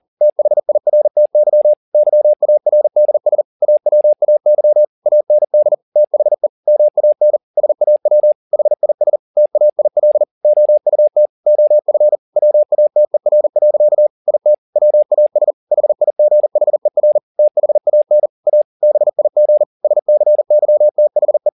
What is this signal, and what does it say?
Morse code, heard as a keyed tone, about 40 wpm, 600 Hz. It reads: THIRTY YARDS AWAY AND THE MAN SAW HIS TAIL OUT OF WATER. IT WAS HIGHER THAN A BIG SCYTHE
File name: the old man and the sea 2 40wpm